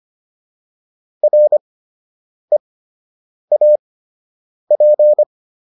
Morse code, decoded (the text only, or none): REAP